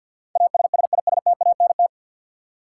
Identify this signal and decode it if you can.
Morse code, heard as a keyed tone, about 48 wpm, 700 Hz. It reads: ASSISTANT
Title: Practice QSO - Exchange 3 - More Info - 50wpm